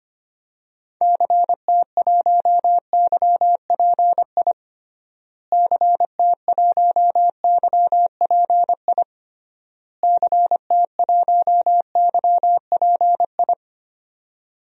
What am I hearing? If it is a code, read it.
Morse code, heard as a keyed tone, about 25 wpm, 700 Hz. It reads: CT1YPI CT1YPI CT1YPI
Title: DXCC-100 Call Signs - 6 Characters - Repeated 3x - 25wpm